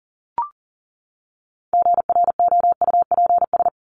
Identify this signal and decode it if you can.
Morse code, heard as a keyed tone, about 40 wpm, 700 Hz. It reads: GROUPS
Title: ES:S Suffix - Top 100 Words with Base Words 40wpm